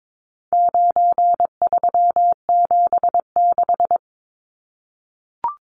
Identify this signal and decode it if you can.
Morse code, heard as a keyed tone, about 22 wpm, 700 Hz. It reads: 9376
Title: Numbers - 4-Digit 22wpm